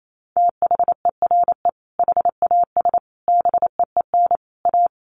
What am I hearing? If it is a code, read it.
Morse code, heard as a keyed tone, about 28 wpm, 700 Hz. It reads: THERE HAS BEEN A
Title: Sets of 4 Words 28wpm